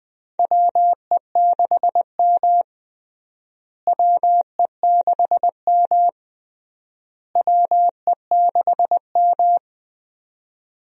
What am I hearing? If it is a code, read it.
Morse code, heard as a keyed tone, about 20 wpm, 700 Hz. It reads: WE6M WE6M WE6M
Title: USA Call Signs - 4 Characters - Repeated 3x - 20wpm